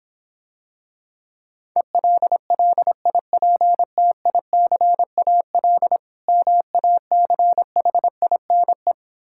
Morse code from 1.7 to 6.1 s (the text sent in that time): ELLIPTICAL